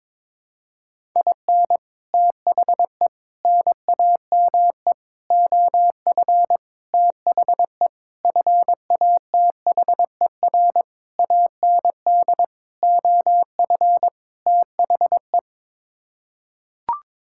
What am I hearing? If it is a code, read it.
Morse code, heard as a keyed tone, about 22 wpm, 700 Hz. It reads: IN THE NAME OF THE FATHER AND OF THE
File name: Sets of 9 Words - 22wpm